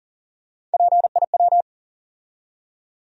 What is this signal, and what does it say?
Morse code, heard as a keyed tone, about 40 wpm, 700 Hz. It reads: PIW
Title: USA Call Sign Postfixes - 3 Letters 40wpm